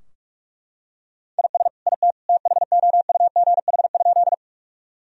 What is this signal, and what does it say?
Morse code, heard as a keyed tone, about 41 wpm, 700 Hz. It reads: IS IT THOUGH?
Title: Sentences from Top 1000 Words - Review Part 2 - 45wpm